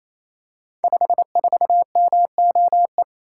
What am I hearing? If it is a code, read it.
Morse code, heard as a keyed tone, about 27 wpm, 700 Hz. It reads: 54MOE
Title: Code Groups - 5 Letter:Numbers - 28wpm